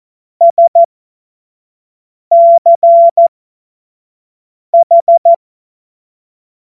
Morse code, heard as a keyed tone, about 14 wpm, spaced as for 5 wpm, 680 Hz. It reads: SCH